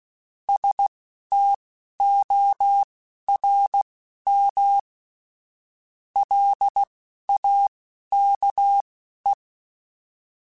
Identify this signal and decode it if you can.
Morse code, heard as a keyed tone, about 16 wpm, spaced as for 11 wpm, 800 Hz. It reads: STORM LAKE